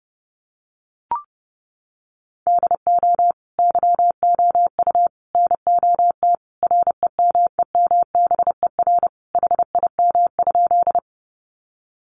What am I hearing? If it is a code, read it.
Morse code, heard as a keyed tone, about 30 wpm, 700 Hz. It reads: DO YOU NOT REMEMBER HIM?
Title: Sentences from Top 600 Words 30wpm